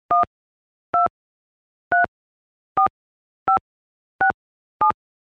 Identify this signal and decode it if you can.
DTMF, heard as telephone touch tones, tones of about 120 ms. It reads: 1234567
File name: Tone dialing sounds on the handset